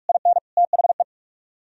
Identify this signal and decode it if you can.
Morse code, heard as a keyed tone, about 63 wpm, spaced as for 53 wpm, 700 Hz. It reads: IN THE
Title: Speed-Racing - Sets of 3 Words 45wpm